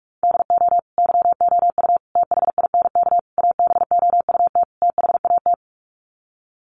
Morse code, heard as a keyed tone, about 44 wpm, 700 Hz. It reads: DO YOU THINK ABOUT THAT
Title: Sets of 5 Words Encore2 45wpm